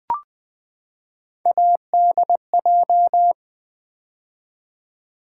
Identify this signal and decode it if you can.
Morse code, heard as a keyed tone, about 20 wpm, 700 Hz. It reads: ADJ
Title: Build A Word - 3 Letters At A Time - 6-Letter Words - 20wpm